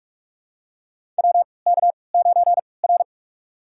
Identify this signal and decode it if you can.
Morse code, heard as a keyed tone, about 44 wpm, spaced as for 25 wpm, 690 Hz. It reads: WK9R
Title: USA Call Signs - 4 Characters - 3x Character Spacing - 45wpm